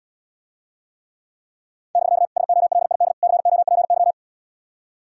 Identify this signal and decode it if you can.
Morse code, heard as a keyed tone, about 80 wpm, 700 Hz. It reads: <BK> SOLID COPY